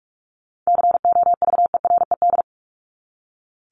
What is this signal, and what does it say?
Morse code, heard as a keyed tone, about 44 wpm, 700 Hz. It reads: COVERED
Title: ED Suffix - Top 100 Words with Base Words 45wpm